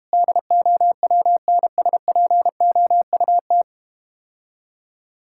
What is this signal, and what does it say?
Morse code, heard as a keyed tone, about 32 wpm, 700 Hz. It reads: DOWNSPOUT